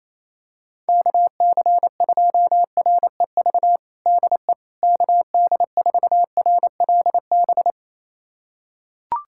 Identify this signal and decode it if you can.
Morse code, heard as a keyed tone, about 28 wpm, 700 Hz. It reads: KC2REV DE KD4RLB
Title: USA Callsign DE Callsign - 6 Characters - 28wpm